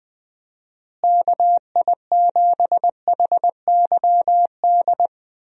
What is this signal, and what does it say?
Morse code, heard as a keyed tone, about 21 wpm, 700 Hz. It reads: KI7HYD
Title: USA Call Signs - 6 Characters - Repeated 3x - 20wpm